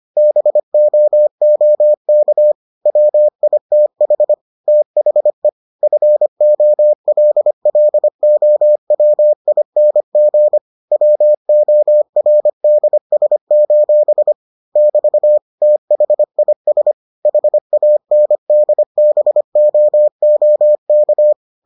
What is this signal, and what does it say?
Morse code, heard as a keyed tone, about 25 wpm, 600 Hz. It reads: BOOK WITH THE FOLLOWING WORDS: <BT> THIS HANDBOOK